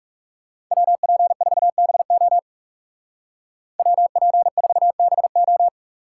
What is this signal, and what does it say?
Morse code, heard as a keyed tone, about 39 wpm, 700 Hz. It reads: WP4BO WP4BO